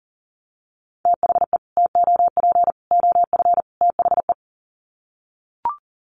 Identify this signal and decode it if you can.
Morse code, heard as a keyed tone, about 40 wpm, 700 Hz. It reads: THE TOP OF THE
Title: Sets of 4 Words 40wpm